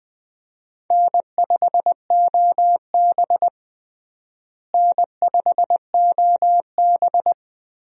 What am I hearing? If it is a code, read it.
Morse code, heard as a keyed tone, about 20 wpm, 700 Hz. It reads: N5OB N5OB
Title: USA Call Signs - 4 Characters - Repeated 2x - 20wpm